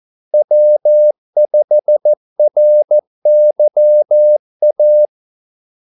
Morse code, heard as a keyed tone, about 14 wpm, 600 Hz. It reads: W5RYA
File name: Callsigns-14wpm